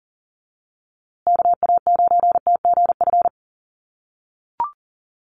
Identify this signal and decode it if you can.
Morse code, heard as a keyed tone, about 40 wpm, 700 Hz. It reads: KA9TGF
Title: USA Call Signs Repeated 2x 40wpm